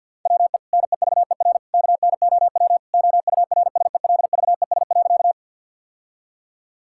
Morse code, heard as a keyed tone, about 50 wpm, 700 Hz. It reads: WE NEVER KNOW OURSELVES.
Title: Sentences from Top 1600 Words 50wpm